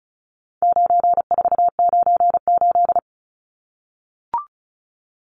Morse code, heard as a keyed tone, about 34 wpm, 700 Hz. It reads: 9498
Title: Numbers - 4-Digit 35wpm